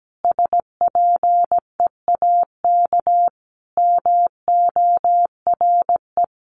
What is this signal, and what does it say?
Morse code, heard as a keyed tone, about 17 wpm, 700 Hz. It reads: SPEAK MORE
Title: Sentences from Top 1300 Words 17wpm